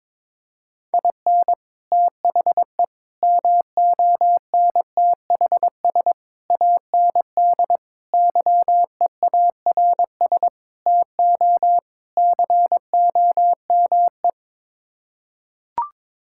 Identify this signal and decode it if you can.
Morse code, heard as a keyed tone, about 22 wpm, 700 Hz. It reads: IN THE MONTHS AND YEARS TO COME
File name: practice / 22wpm / Sets of 7 Words - 22wpm